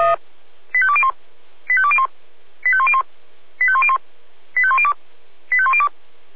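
Five-tone selective call (ZVEI), heard as quick runs of five tones, tones of about 70 ms. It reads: 85201 85201 85101 85101 85102 85102
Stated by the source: ZVEI five-tone selective call received by SDR.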